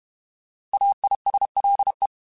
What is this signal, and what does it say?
Morse code, heard as a keyed tone, about 31 wpm, 800 Hz. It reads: AISLE
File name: Alphabetical / Alphabetical mp3s / As @30